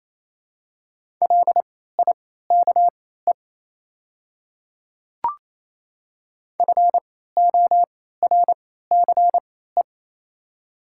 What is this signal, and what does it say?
Morse code, heard as a keyed tone, about 28 wpm, spaced as for 16 wpm, 700 Hz. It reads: LIKE FORCE
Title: 3x Character Spacing - Top 500 Words - 28wpm